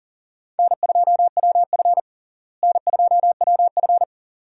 Morse code, heard as a keyed tone, about 37 wpm, 700 Hz. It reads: N2WF N2WF
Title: USA Call Signs - 4 Characters - Repeated 2x - 40wpm